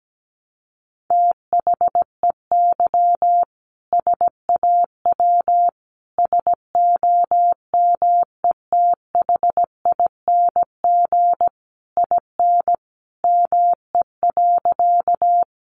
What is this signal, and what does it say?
Morse code, heard as a keyed tone, about 17 wpm, 700 Hz. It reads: THEY SAW SOMETHING IN ME.
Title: Sentences from Top 600 Words 17wpm